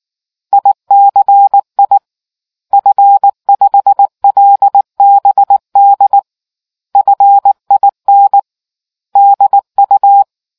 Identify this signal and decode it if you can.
Morse code, heard as a keyed tone, about 19 wpm, 800 Hz. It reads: ICI F5LBD FIN DU